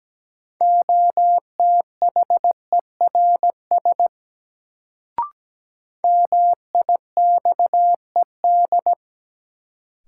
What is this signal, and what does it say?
Morse code, heard as a keyed tone, about 17 wpm, 700 Hz. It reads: OTHERS MIXED
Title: Sentences from Top 1500 Words - New Word Drill Alt - 17wpm